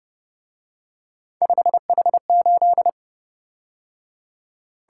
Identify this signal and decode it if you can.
Morse code, heard as a keyed tone, about 29 wpm, 700 Hz. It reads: 5H8